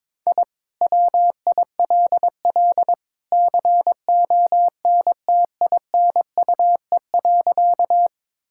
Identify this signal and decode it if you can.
Morse code, heard as a keyed tone, about 22 wpm, 700 Hz. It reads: I WILL CONTINUE.
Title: Sentences from Top 1000 Words - Review Part 6 - 22wpm